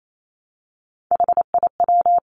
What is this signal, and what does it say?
Morse code, heard as a keyed tone, about 28 wpm, 700 Hz. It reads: HIW
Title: Three Challenging Letter-Number Combinations 28wpm